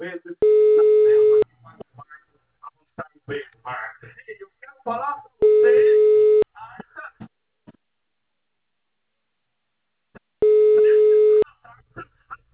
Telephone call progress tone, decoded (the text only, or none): ringback tone